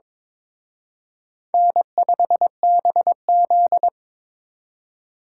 Morse code, heard as a keyed tone, about 26 wpm, 700 Hz. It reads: N5BZ
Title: Speed Racing - USA 1x2 Call Signs 22wpm